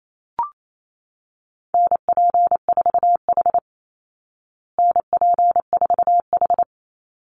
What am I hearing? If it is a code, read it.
Morse code, heard as a keyed tone, about 28 wpm, 700 Hz. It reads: NP4H NP4H
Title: Call Signs Repeated 3x - 28wpm